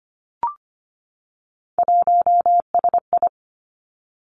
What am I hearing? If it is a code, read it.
Morse code, heard as a keyed tone, about 25 wpm, 700 Hz. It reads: 1SI
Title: Three Challenging Letter-Number Combinations 25wpm